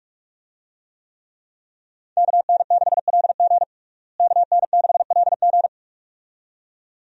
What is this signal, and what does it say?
Morse code, heard as a keyed tone, about 46 wpm, 700 Hz. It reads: KN6LG KN6LG
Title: USA Call Signs - 5 Characters - Repeated 2x - 45wpm